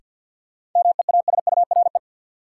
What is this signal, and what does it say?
Morse code, heard as a keyed tone, about 53 wpm, 700 Hz. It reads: MEASURE
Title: Sentences from Top 1200 Words - New Word Drill Alt - 50wpm